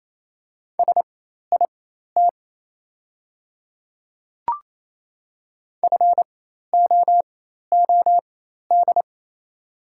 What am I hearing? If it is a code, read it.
Morse code, heard as a keyed tone, about 28 wpm, spaced as for 13 wpm, 700 Hz. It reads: SIT FOOD
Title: 4x Character Spacing - Top 500 Words - 28wpm